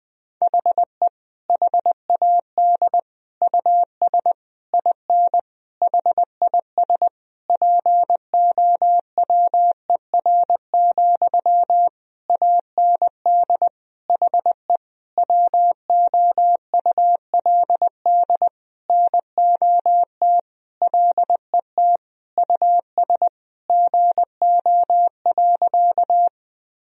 Morse code, heard as a keyed tone, about 20 wpm, 700 Hz. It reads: HE HAD US IN HIS POWER, AND HE WOULD NOT LET US GO.
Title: Sentences from Top 1000 Words - Review Part 2 - 20wpm